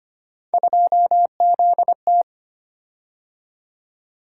Morse code, heard as a keyed tone, about 25 wpm, 700 Hz. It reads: 2ZT
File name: Three Letter-Number Combinations - 25wpm